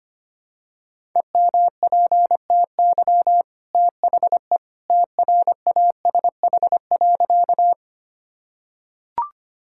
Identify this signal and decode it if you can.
Morse code, heard as a keyed tone, about 25 wpm, 700 Hz. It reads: EMPTY THE TRASH.
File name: Easy Sentences with Hints - 25wpm